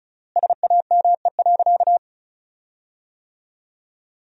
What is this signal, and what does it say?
Morse code, heard as a keyed tone, about 36 wpm, 700 Hz. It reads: SAME.
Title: 3x Word-Spacing - Sentences from Top 300 Words - 35wpm